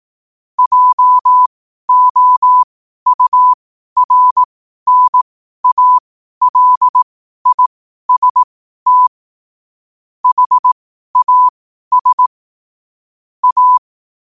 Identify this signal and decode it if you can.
Morse code, heard as a keyed tone, about 18 wpm, spaced as for 12 wpm, 1000 Hz. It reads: JOURNALIST HAS A
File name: qotdfiles / 12wpm